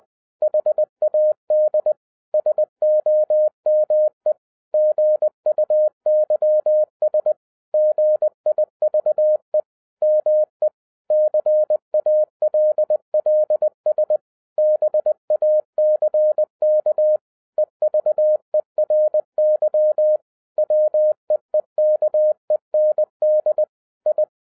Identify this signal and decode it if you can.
Morse code, heard as a keyed tone, about 20 wpm, 610 Hz. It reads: HAD SOME GUYS GIVE ME CALLS BACK EVERY WEEKEND I
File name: 2012-01-mp3 / n16-THROUGH EIGHT MONTHS-20wpm